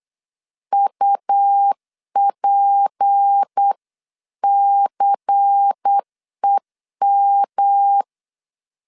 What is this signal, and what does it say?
Morse code, heard as a keyed tone, about 8 wpm, 800 Hz. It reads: UPCEM
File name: 8wpm